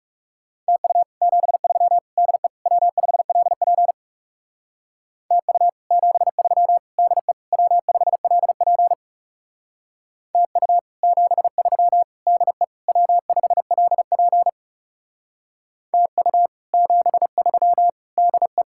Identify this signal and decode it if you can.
Morse code, heard as a keyed tone, about 38 wpm, 700 Hz. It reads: TU 73 DE W5LP TU 73 DE W5LP TU 73 DE W5LP TU 73 DE